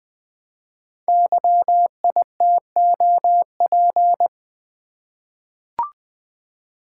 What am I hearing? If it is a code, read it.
Morse code, heard as a keyed tone, about 20 wpm, 700 Hz. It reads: YITOP
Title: Code Groups - 5 Letters - 20wpm